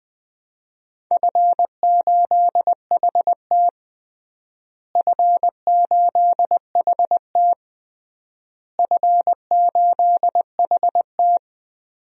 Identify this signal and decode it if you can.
Morse code, heard as a keyed tone, about 20 wpm, 700 Hz. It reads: F8HT F8HT F8HT
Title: DXCC-100 Call Signs - 4 Characters - Repeated 3x - 20wpm